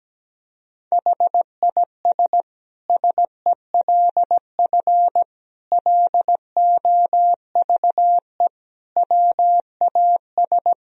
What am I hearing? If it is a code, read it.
Morse code, heard as a keyed tone, about 17 wpm, 700 Hz. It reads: HIS SELF LOVE WAS
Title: Sentences from Top 1000 Words - Review Part 1 - 17wpm